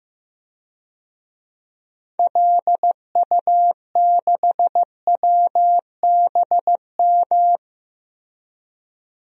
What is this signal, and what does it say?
Morse code, heard as a keyed tone, about 15 wpm, 700 Hz. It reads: LU6WBM
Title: DXCC-100 Call Signs - 6 Characters - Repeated 2x - 15wpm